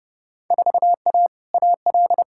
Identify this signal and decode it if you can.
Morse code, heard as a keyed tone, about 31 wpm, 700 Hz. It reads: 4A AL
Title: Field Day Exchanges - 30wpm